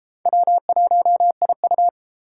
Morse code, heard as a keyed tone, about 30 wpm, 700 Hz. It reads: W1IU